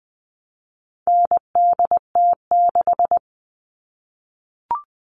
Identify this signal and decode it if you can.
Morse code, heard as a keyed tone, about 20 wpm, 700 Hz. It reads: NDT6
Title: All Cut Numbers - 4-Digit 20wpm